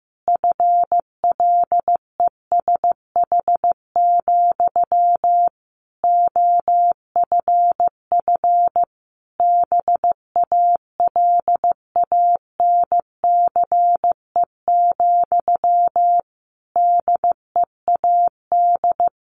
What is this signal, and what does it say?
Morse code, heard as a keyed tone, about 15 wpm, 700 Hz. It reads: FLESH, OFF BALANCE, DEAD
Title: A Sound Of Thunder 15wpm